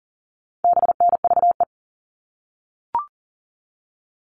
Morse code, heard as a keyed tone, about 40 wpm, 700 Hz. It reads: BNVE
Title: All Cut Numbers - 4-Digit 40wpm